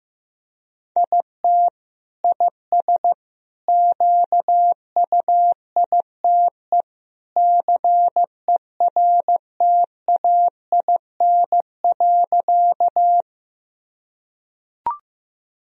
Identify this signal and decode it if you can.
Morse code, heard as a keyed tone, about 15 wpm, 700 Hz. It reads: IT IS QUITE CERTAIN.
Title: Sentences from Top 1000 Words - Review Part 2 - 15wpm